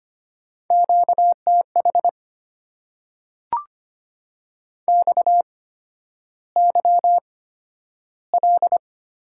Morse code, heard as a keyed tone, about 25 wpm, spaced as for 11 wpm, 700 Hz. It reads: QTH XYL
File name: Top 90 QSO Elements - 8x Character Spacing - 25wpm